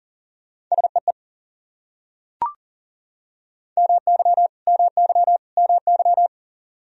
Morse code, heard as a keyed tone, about 40 wpm, 700 Hz. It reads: SEE MY MY MY
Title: Top 100 Words Repeated 40wpm